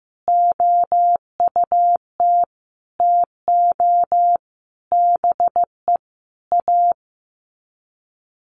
Morse code, heard as a keyed tone, about 15 wpm, 700 Hz. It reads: OUT TO BE A
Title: Sets of 5 Words 15wpm